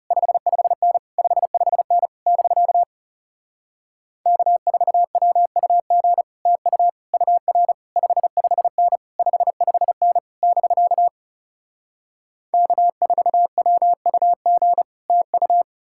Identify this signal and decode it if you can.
Morse code, heard as a keyed tone, about 40 wpm, 700 Hz. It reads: 55N 55N <BK> K4WUG TU UR 55N 55N <BK> K4WUG TU